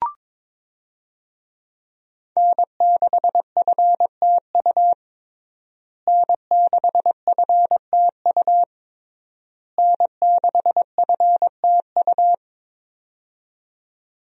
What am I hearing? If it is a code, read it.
Morse code, heard as a keyed tone, about 22 wpm, 700 Hz. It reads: N6FTU N6FTU N6FTU
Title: USA Call Signs - 5 Characters - Repeated 3x - 22wpm